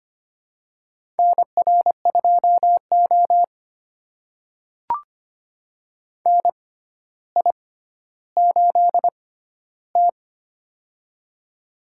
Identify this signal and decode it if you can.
Morse code, heard as a keyed tone, about 25 wpm, spaced as for 13 wpm, 700 Hz. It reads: NR2O NI8T
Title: USA Call Signs - 6x Character Spacing - 25wpm